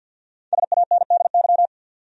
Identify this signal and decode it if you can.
Morse code, heard as a keyed tone, about 52 wpm, 700 Hz. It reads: SANDY
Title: USA Common Names - 50wpm